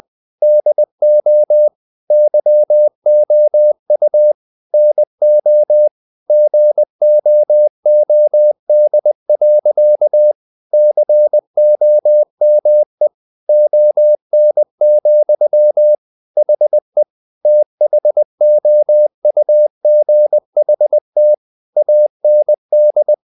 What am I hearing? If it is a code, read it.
Morse code, heard as a keyed tone, about 20 wpm, 600 Hz. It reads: DO YOU NO GOOD. COME ON, HE THOUGHT AND
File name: the old man and the sea / the old man and the sea part 1 20wpm